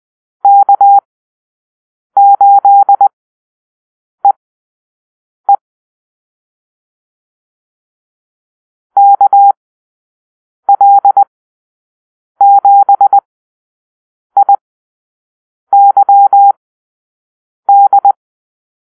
Morse code, heard as a keyed tone, about 20 wpm, spaced as for 6 wpm, 800 Hz. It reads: K8EE KL7IYD
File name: arrl / Call-Sign Drill-6